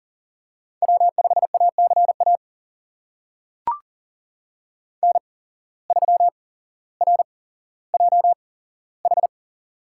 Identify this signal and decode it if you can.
Morse code, heard as a keyed tone, about 40 wpm, spaced as for 17 wpm, 700 Hz. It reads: W5ACA N3RJH
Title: USA Call Signs - 8x Character Spacing - 40wpm